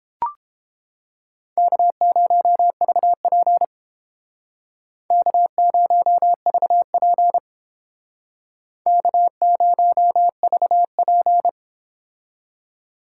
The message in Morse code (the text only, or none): K0VP K0VP K0VP